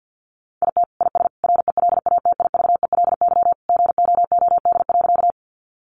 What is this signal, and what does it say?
Morse code, heard as a keyed tone, about 50 wpm, 700 Hz. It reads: IT IS RELATIVELY GOOD.
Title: Sentences from Top 1100 Words 50wpm